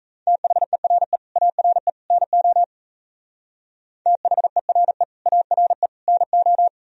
Morse code, heard as a keyed tone, about 37 wpm, 700 Hz. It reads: THERE ARE NO THERE ARE NO